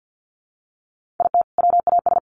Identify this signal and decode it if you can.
Morse code, heard as a keyed tone, about 50 wpm, 700 Hz. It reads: IT WAS
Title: Sentences from Top 1400 Words 50wpm